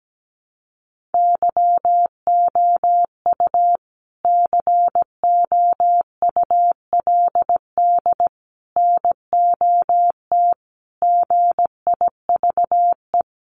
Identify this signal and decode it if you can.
Morse code, heard as a keyed tone, about 17 wpm, 700 Hz. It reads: YOU COULD NOT GIVE
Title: Sentences from Top 1500 Words 17wpm